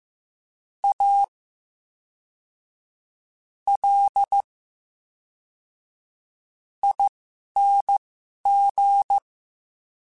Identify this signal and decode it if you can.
Morse code, heard as a keyed tone, about 15 wpm, spaced as for 11 wpm, 800 Hz. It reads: A L ING
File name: Special INGs / Special INGs-mp3s / VerySlowToFaster Ing @10